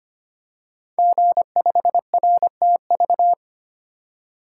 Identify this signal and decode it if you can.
Morse code, heard as a keyed tone, about 25 wpm, 700 Hz. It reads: G5RTV
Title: Code Groups - 5 Letter:Numbers - 25wpm